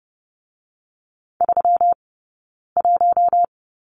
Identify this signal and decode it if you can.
Morse code, heard as a keyed tone, about 30 wpm, 700 Hz. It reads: 3 1
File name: Lesson 20d - Number 3 - Two Characters 30wpm